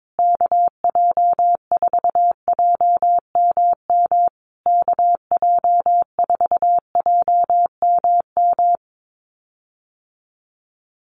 Morse code, heard as a keyed tone, about 22 wpm, 700 Hz. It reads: KJ4JMM KJ4JMM
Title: USA Call Signs Repeated 3x 22wpm